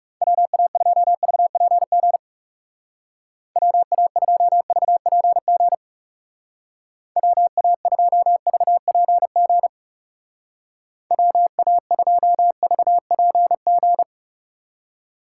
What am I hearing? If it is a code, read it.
Morse code, heard as a keyed tone, about 36 wpm, 700 Hz. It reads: WA2VPG WA2VPG WA2VPG WA2VPG
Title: Speed Racing - USA 2x3 Call Signs 30wpm